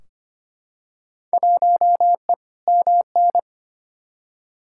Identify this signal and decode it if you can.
Morse code, heard as a keyed tone, about 25 wpm, 700 Hz. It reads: 1E MN
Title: Field Day Exchanges - 25wpm